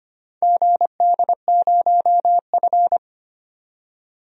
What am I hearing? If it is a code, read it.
Morse code, heard as a keyed tone, about 25 wpm, 700 Hz. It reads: GD0F
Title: DXCC-100 Call Signs - 4 Characters - 25wpm